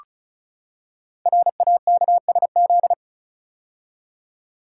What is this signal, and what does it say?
Morse code, heard as a keyed tone, about 36 wpm, 700 Hz. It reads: RAKSZ
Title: Code Groups - 5 Letter:Numbers - 35wpm